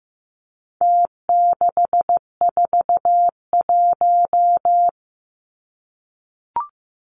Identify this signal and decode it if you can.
Morse code, heard as a keyed tone, about 15 wpm, 700 Hz. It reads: T641
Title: Numbers - Contest Style - 4-Digit 15wpm